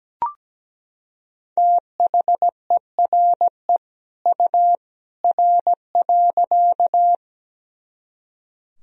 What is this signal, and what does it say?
Morse code, heard as a keyed tone, about 17 wpm, 700 Hz. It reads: THERE U R.
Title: Sentences from Top 100 Words and Abbreviations 17wpm